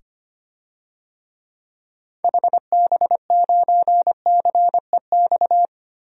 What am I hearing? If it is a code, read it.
Morse code, heard as a keyed tone, about 25 wpm, 700 Hz. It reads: HB9CEX
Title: DXCC-100 Call Signs - 6 Characters - 25wpm